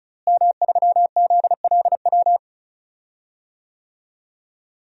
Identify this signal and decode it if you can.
Morse code, heard as a keyed tone, about 34 wpm, 700 Hz. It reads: M3ZLW
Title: DXCC-100 Call Signs - 5 Characters - Repeated 2x - 35wpm